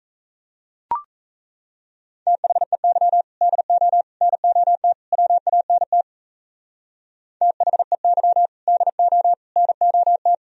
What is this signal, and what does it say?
Morse code, heard as a keyed tone, about 35 wpm, 700 Hz. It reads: THEY DO NOT WANT THEY DO NOT